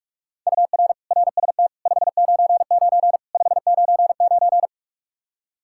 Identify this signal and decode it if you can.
Morse code, heard as a keyed tone, about 47 wpm, 700 Hz. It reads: UR RST 599 599
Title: K1USN SST QSO Practice 45wpm